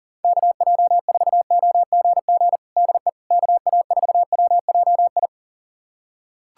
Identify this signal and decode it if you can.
Morse code, heard as a keyed tone, about 37 wpm, 700 Hz. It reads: KJ4OGG DE KA4WJI
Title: USA Callsign DE Callsign - 6 Characters - 40wpm